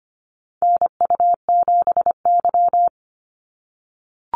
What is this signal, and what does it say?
Morse code, heard as a keyed tone, about 25 wpm, 690 Hz. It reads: NU7Y
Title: NA SOTA Activator Callsigns 25wpm